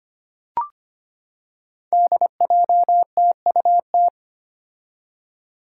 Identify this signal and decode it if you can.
Morse code, heard as a keyed tone, about 25 wpm, 700 Hz. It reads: DJTUT
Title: Five Challenging Letter-Number Combinations 25wpm